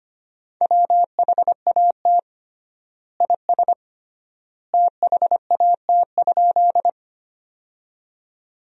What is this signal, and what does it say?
Morse code, heard as a keyed tone, about 25 wpm, 700 Hz. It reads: WHAT IS THAT?
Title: 3x Word-Spacing - Sentences from Top 100 Words - 25wpm